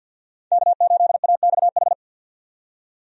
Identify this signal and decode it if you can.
Morse code, heard as a keyed tone, about 52 wpm, 700 Hz. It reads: K8AXH
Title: USA Call Signs - 5 Characters - 50wpm